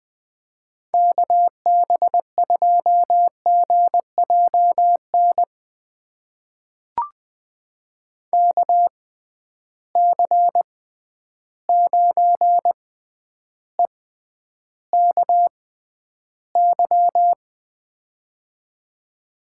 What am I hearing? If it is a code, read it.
Morse code, heard as a keyed tone, about 20 wpm, spaced as for 10 wpm, 700 Hz. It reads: KB2GJN KC9EKY
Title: USA Call Signs - 6 Characters - 6x Character Spacing - 20wpm